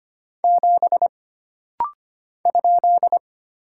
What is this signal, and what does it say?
Morse code, heard as a keyed tone, about 25 wpm, 700 Hz. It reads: 7 ?
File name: Single Character (Letter:Number:Punctuation:Prosign) - Rapid-Fire - 25wpm